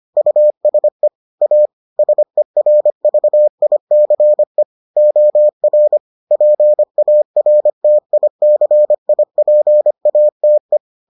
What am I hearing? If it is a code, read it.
Morse code, heard as a keyed tone, about 25 wpm, 600 Hz. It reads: USE A SERVICE OR PARTICIPATE